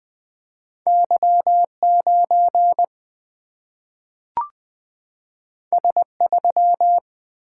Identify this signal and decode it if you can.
Morse code, heard as a keyed tone, about 20 wpm, 700 Hz. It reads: Y9 S3
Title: Single-Letter Single-Number 20wpm